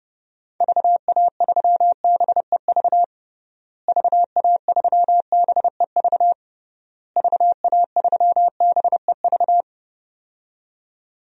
Morse code, heard as a keyed tone, about 30 wpm, 700 Hz. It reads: VA3BEV VA3BEV VA3BEV
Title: DXCC-100 Call Signs - 6 Characters - Repeated 3x - 30wpm